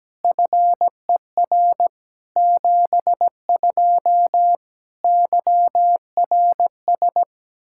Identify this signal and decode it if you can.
Morse code, heard as a keyed tone, about 17 wpm, 700 Hz. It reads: FER 72 YRS
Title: Practice QSO - Exchange 3 - More Info - 17wpm